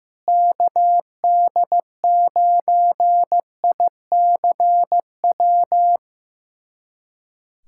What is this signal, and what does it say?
Morse code, heard as a keyed tone, about 15 wpm, 700 Hz. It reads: KD9ICW
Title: USA Callsign DE Callsign 15wpm